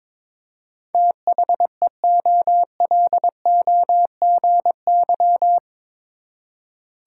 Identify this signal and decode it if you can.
Morse code, heard as a keyed tone, about 22 wpm, 700 Hz. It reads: THEOLOGY
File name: practice / 22wpm / Eight Letter Words - 22wpm